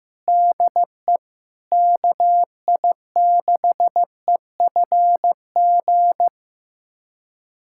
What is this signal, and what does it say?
Morse code, heard as a keyed tone, about 15 wpm, 700 Hz. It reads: DE KI6EFG
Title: USA Callsign DE Callsign 15wpm